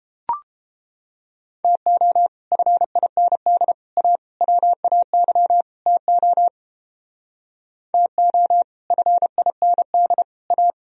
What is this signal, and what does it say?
Morse code, heard as a keyed tone, about 29 wpm, 700 Hz. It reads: TO FIND A WAY TO TO FIND A